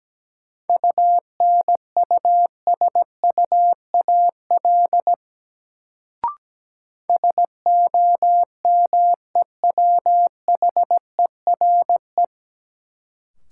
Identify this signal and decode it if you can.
Morse code, heard as a keyed tone, about 17 wpm, 700 Hz. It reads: UNUSUAL SOMEWHERE
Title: Sentences from Top 1400 Words - New Word Drill - 17wpm